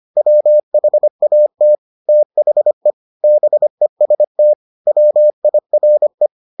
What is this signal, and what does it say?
Morse code, heard as a keyed tone, about 25 wpm, 600 Hz. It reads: WHAT THE BEST WIRE